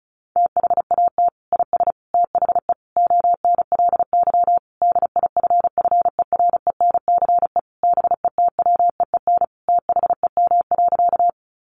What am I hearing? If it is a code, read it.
Morse code, heard as a keyed tone, about 34 wpm, 700 Hz. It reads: THAT IS THE ONLY DIFFERENCE BETWEEN THEM.
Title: Sentences from Top 800 Words 35wpm